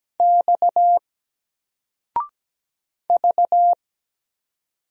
Morse code, heard as a keyed tone, about 17 wpm, 700 Hz. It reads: X V
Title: Single Letter-Number 17wpm